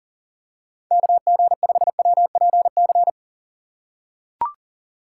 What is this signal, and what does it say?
Morse code, heard as a keyed tone, about 40 wpm, 700 Hz. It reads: KG5WPC
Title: USA Call Signs - 6 Characters - 4x Character Spacing - 40wpm